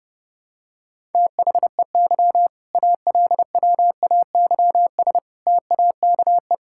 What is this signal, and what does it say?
Morse code, heard as a keyed tone, about 32 wpm, 700 Hz. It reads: THEY ALWAYS TAKE
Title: Sentences from Top 1000 Words - Review Part 6 - 30wpm